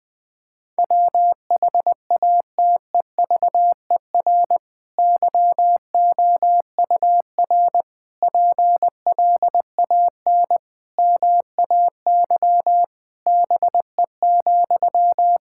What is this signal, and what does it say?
Morse code, heard as a keyed tone, about 20 wpm, 700 Hz. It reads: WHATEVER YOUR PLAN MAY BE,
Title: Sentences from Top 1600 Words 20wpm